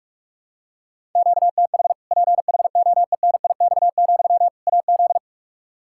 Morse code, heard as a keyed tone, about 41 wpm, 700 Hz. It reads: QTH PHOENIX, AZ
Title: Practice QSO - Exchange 1 - Beginning - 45wpm